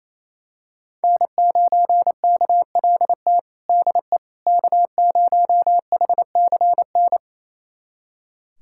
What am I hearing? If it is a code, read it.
Morse code, heard as a keyed tone, about 28 wpm, 700 Hz. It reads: N9KLT DE K0HCN
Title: USA Callsign DE Callsign - 5 Characters - 28wpm